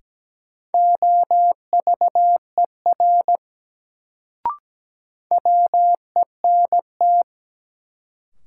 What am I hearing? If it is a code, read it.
Morse code, heard as a keyed tone, about 17 wpm, 700 Hz. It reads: OVER WENT
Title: Sets of 2 Words - New Word Drill - 17wpm